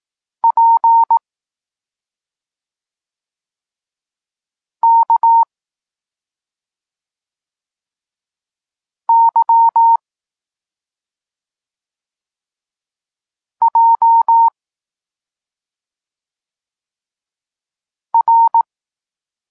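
Morse code, heard as a keyed tone, about 18 wpm, 920 Hz. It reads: P K Y J R
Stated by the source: Polish Morse lesson teaching the letter C, from SP9MOA Niepołomicki Klub Krótkofalowców.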